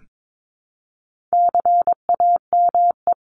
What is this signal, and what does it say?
Morse code, heard as a keyed tone, about 22 wpm, 700 Hz. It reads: CAME
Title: Top 200 Words Repeated 22wpm